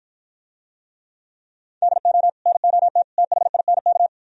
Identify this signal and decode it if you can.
Morse code, heard as a keyed tone, about 52 wpm, 700 Hz. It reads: DO NOT THINK